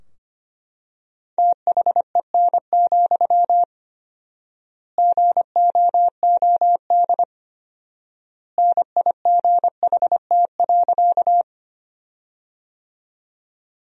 Morse code, heard as a keyed tone, about 25 wpm, 700 Hz. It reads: THEN, GOOD NIGHT.
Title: 4x Word-Spacing - Sentences from Top 300 Words - 25wpm